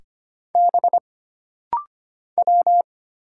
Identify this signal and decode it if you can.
Morse code, heard as a keyed tone, about 25 wpm, 700 Hz. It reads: B W
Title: Single Letters - Rapid-Fire 25wpm